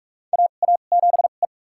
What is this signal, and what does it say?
Morse code, heard as a keyed tone, about 46 wpm, spaced as for 32 wpm, 700 Hz. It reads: AA7E
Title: USA Call Signs - 2x Character Spacing - 45wpm